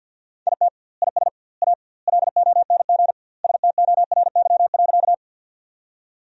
Morse code, heard as a keyed tone, about 52 wpm, 700 Hz. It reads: IT IS A LONG STORY.
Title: 2x Word-Spacing - Sentences from Top 300 Words - 50wpm